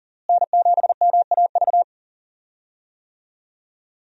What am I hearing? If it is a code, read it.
Morse code, heard as a keyed tone, about 40 wpm, 700 Hz. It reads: N7MAV
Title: USA Call Signs - 5 Characters - Repeated 3x - 40wpm